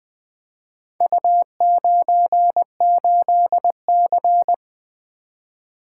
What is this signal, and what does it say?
Morse code, heard as a keyed tone, about 21 wpm, 700 Hz. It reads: U98C
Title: Four Letter-Number Combinations - 20wpm